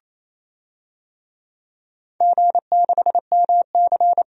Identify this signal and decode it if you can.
Morse code, heard as a keyed tone, about 27 wpm, 700 Hz. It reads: G6MC
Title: DXCC-100 Call Signs - 4 Characters - 28wpm